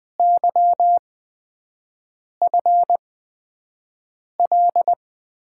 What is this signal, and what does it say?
Morse code, heard as a keyed tone, about 20 wpm, spaced as for 5 wpm, 700 Hz. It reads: YFL
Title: USA Call Signs - 8x Character Spacing - 20wpm